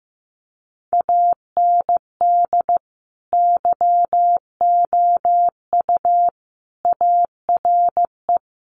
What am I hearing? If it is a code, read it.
Morse code, heard as a keyed tone, about 15 wpm, 700 Hz. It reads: AND YOU ARE
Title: Sentences from Top 1300 Words 15wpm